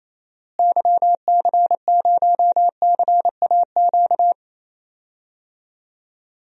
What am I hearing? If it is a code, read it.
Morse code, heard as a keyed tone, about 28 wpm, 700 Hz. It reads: YC0CAQ
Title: DXCC-100 Call Signs - 6 Characters - Repeated 2x - 28wpm